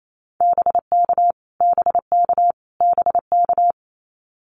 Morse code, heard as a keyed tone, about 28 wpm, 700 Hz. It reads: BK BK BK
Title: Abbreviations Repeated 3x - 28wpm